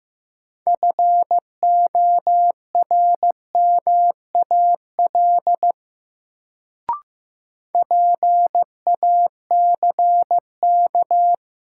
Morse code, heard as a keyed tone, about 15 wpm, 700 Hz. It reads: FORMAL PACK
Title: Sentences from Top 1300 Words - New Word Drill - 15wpm